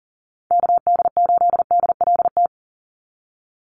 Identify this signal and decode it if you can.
Morse code, heard as a keyed tone, about 40 wpm, 700 Hz. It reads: KD8DLT
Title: USA Call Signs 40wpm